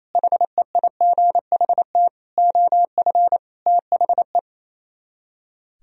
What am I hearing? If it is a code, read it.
Morse code, heard as a keyed tone, about 36 wpm, 700 Hz. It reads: HEIGHT OF THE